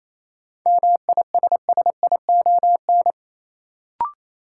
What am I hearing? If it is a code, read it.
Morse code, heard as a keyed tone, about 28 wpm, 700 Hz. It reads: MISSION
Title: Sentences from Top 1600 Words - New Word Drill Alt - 28wpm